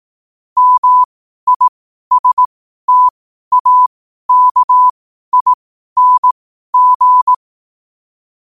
Morse code, heard as a keyed tone, about 18 wpm, spaced as for 12 wpm, 1000 Hz. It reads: MISTAKING
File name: qotdfiles / 12wpm